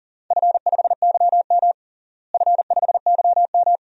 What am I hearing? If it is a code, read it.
Morse code, heard as a keyed tone, about 40 wpm, 700 Hz. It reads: F5YM F5YM
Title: DXCC-100 Call Signs - 4 Characters - Repeated 2x - 40wpm